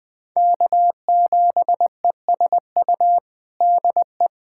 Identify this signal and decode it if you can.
Morse code, heard as a keyed tone, about 20 wpm, 700 Hz. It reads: K7ESU DE
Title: USA Callsign DE Callsign - 5 Characters - 20wpm